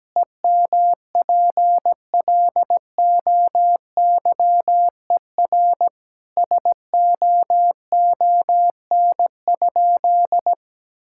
Morse code, heard as a keyed tone, about 17 wpm, 700 Hz. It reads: EMPLOYER SOON?
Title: Sentences from Top 1600 Words 17wpm